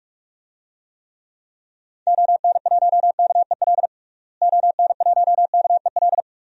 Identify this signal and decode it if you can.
Morse code, heard as a keyed tone, about 46 wpm, 700 Hz. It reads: ON1KEL ON1KEL
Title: DXCC-100 Call Signs - 6 Characters - Repeated 3x - 45wpm